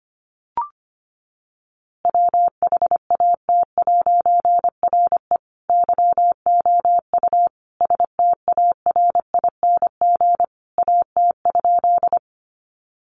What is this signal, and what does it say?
Morse code, heard as a keyed tone, about 25 wpm, 700 Hz. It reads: WHAT'RE YOU STARING AT?
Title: A Sound Of Thunder 25wpm